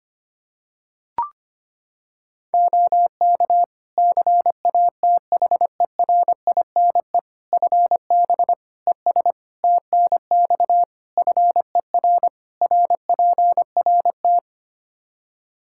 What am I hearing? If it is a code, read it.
Morse code, heard as a keyed tone, about 25 wpm, 700 Hz. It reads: OK CATHERINE FB ES TNX FER RPRT
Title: Practice QSO - Exchange 2 - Info - 25wpm